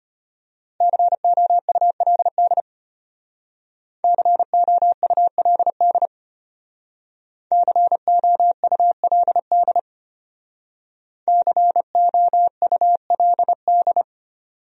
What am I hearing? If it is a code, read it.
Morse code, heard as a keyed tone, about 30 wpm, 700 Hz. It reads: COULD COULD COULD COULD
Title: Speed-Racing Top 200 Words 25wpm